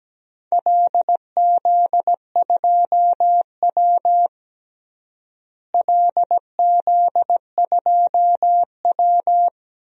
Morse code, heard as a keyed tone, about 17 wpm, 700 Hz. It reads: LZ2W LZ2W
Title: DXCC-100 Call Signs - 4 Characters - Repeated 2x - 17wpm